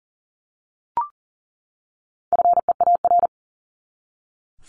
Morse code, heard as a keyed tone, about 39 wpm, 700 Hz. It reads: FEAR
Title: Top 1751-2000 Words 40wpm